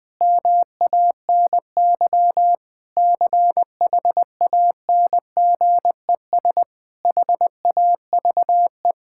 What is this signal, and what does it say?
Morse code, heard as a keyed tone, about 20 wpm, 700 Hz. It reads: MANY CHANGES HAVE
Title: Sentences with 6-Word Phrases - 20wpm